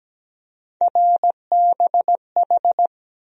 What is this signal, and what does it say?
Morse code, heard as a keyed tone, about 17 wpm, 700 Hz. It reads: RBH
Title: USA Call Sign Postfixes - 3 Letters 17wpm